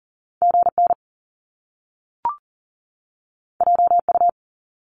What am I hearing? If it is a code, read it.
Morse code, heard as a keyed tone, about 40 wpm, 700 Hz. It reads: GN JU
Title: Top Two Letter Combinations in Words 40wpm